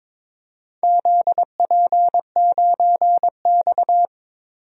Morse code, heard as a keyed tone, about 22 wpm, 700 Hz. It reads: ZP9X
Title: DXCC-100 Call Signs - 4 Characters - Repeated 3x - 22wpm